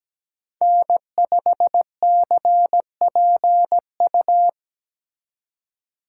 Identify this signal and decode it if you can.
Morse code, heard as a keyed tone, about 17 wpm, 700 Hz. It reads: N5CPU
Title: USA Call Signs - 2x Character Spacing - 17wpm